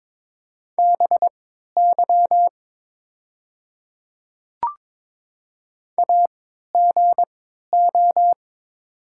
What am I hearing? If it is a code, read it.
Morse code, heard as a keyed tone, about 22 wpm, spaced as for 12 wpm, 690 Hz. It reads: BY AGO
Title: 3x Character Spacing - Top 500 Words - 22wpm